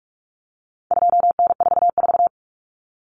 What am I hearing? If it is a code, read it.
Morse code, heard as a keyed tone, about 44 wpm, 700 Hz. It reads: 2N44
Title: Common Cut Numbers - 4-Digit 45wpm